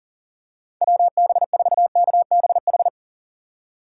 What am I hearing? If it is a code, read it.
Morse code, heard as a keyed tone, about 38 wpm, 700 Hz. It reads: WB4KBH
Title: USA Call Signs - 6 Characters - 2x Character Spacing - 40wpm